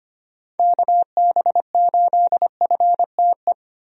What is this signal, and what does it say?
Morse code, heard as a keyed tone, about 25 wpm, 700 Hz. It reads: KB8FTE
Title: USA Call Signs - 6 Characters - 25wpm